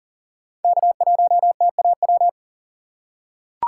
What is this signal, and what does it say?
Morse code, heard as a keyed tone, about 37 wpm, 700 Hz. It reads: K1TAW
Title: USA Call Signs - 3x Character Spacing - 40wpm